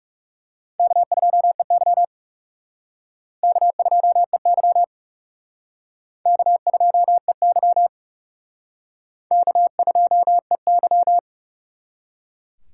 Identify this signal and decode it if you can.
Morse code, heard as a keyed tone, about 37 wpm, 700 Hz. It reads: K2EY K2EY K2EY K2EY
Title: Speed Racing - USA 1x2 Call Signs 30wpm